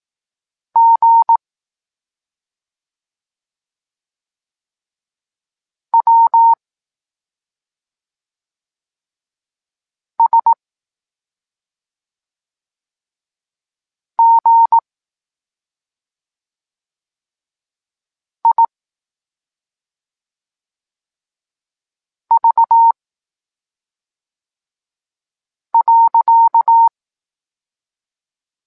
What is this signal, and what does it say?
Morse code, heard as a keyed tone, about 18 wpm, 920 Hz. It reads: G W S G I V .
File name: Tekst-23 4